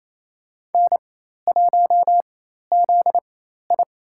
Morse code, heard as a keyed tone, about 28 wpm, spaced as for 13 wpm, 700 Hz. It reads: N1ZI
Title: USA Call Signs - 4 Characters - 4x Character Spacing - 28wpm